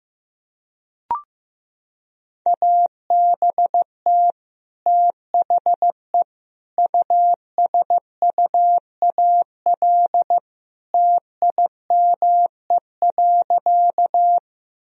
Morse code, heard as a keyed tone, about 15 wpm, 700 Hz. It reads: ABT THE USUAL TIME.
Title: Sentences from Top 400 Words and Abbreviations 15wpm